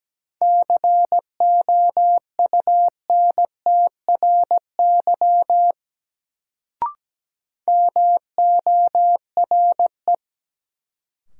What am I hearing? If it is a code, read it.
Morse code, heard as a keyed tone, about 17 wpm, 700 Hz. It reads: COUNTRY MORE
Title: Sets of 2 Words - New Word Drill - 17wpm